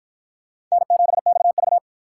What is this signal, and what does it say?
Morse code, heard as a keyed tone, about 53 wpm, 700 Hz. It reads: N7XV